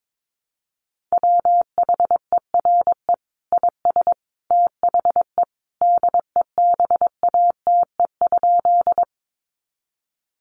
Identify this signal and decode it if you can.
Morse code, heard as a keyed tone, about 22 wpm, 700 Hz. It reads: WHERE IS THE DEBATE?
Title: Sentences from Top 1100 Words 22wpm